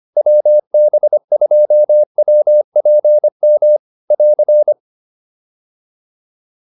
Morse code, heard as a keyed tone, about 25 wpm, 600 Hz. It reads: WB2WPM <AR>
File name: QSO106 25